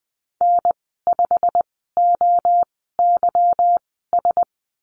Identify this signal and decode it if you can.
Morse code, heard as a keyed tone, about 20 wpm, spaced as for 14 wpm, 700 Hz. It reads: N5OYS
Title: 161 Lesson 26g - Letter g - Call Signs 20wpm